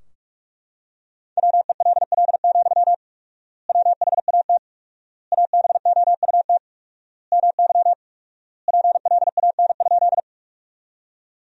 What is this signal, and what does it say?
Morse code, heard as a keyed tone, about 42 wpm, 700 Hz. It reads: WELL, WHAT ABOUT MY PLAN?
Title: 4x Word-Spacing - Sentences from Top 500 Words - 45wpm